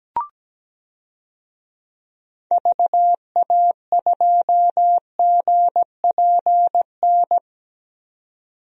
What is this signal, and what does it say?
Morse code, heard as a keyed tone, about 17 wpm, 700 Hz. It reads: VA2GPN
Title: DXCC-100 Call Signs - 6 Characters - 17wpm